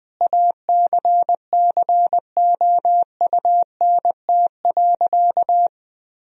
Morse code, heard as a keyed tone, about 21 wpm, 700 Hz. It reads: ACCOUNT.
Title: Sentences from Top 1000 Words - Review Part 4 - 20wpm